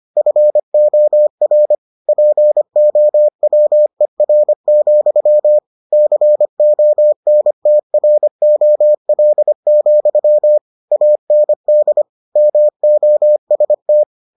Morse code, heard as a keyed tone, about 25 wpm, 600 Hz. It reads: FOR POWER, CONTROL, AND MOST